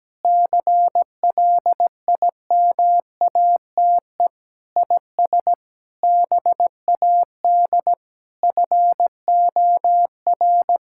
Morse code, heard as a keyed tone, about 17 wpm, 700 Hz. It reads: CLIMATE IS BAD FOR
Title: Sentences from Top 1600 Words 17wpm